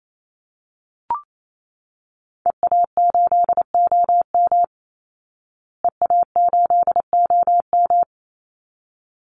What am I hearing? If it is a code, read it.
Morse code, heard as a keyed tone, about 28 wpm, 700 Hz. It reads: EA8OM EA8OM
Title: Call Signs Repeated 3x - 28wpm